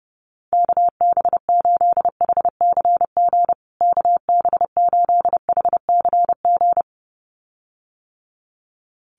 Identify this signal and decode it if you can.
Morse code, heard as a keyed tone, about 30 wpm, 700 Hz. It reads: KB8HCG KB8HCG
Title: USA Call Signs Repeated 2x 30wpm